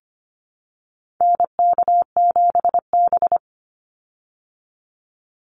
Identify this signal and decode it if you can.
Morse code, heard as a keyed tone, about 25 wpm, 700 Hz. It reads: NK7B
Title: Call Signs - 25wpm